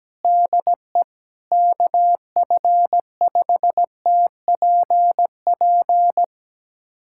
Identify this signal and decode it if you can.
Morse code, heard as a keyed tone, about 17 wpm, 700 Hz. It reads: DE KF5TPP
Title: USA Callsign DE Callsign - 6 Characters - 17wpm